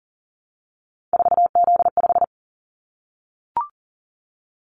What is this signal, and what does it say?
Morse code, heard as a keyed tone, about 40 wpm, 700 Hz. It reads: 4Z5
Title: Callsign Prefixes for Top 50 DXCC - 40wpm